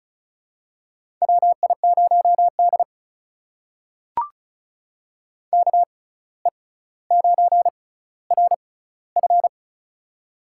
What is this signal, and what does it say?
Morse code, heard as a keyed tone, about 36 wpm, spaced as for 17 wpm, 700 Hz. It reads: WI0D KE9RF
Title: USA Call Signs - 6x Character Spacing - 35wpm